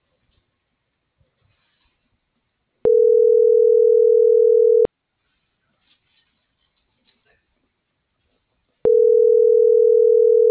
Ringback tone (North America), 440 + 480 Hz.